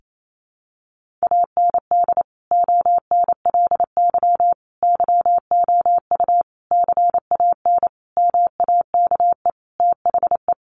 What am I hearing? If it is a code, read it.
Morse code, heard as a keyed tone, about 28 wpm, 700 Hz. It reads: AND ONLY YOU CAN MAKE THE
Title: Sentences from Top 1400 Words 28wpm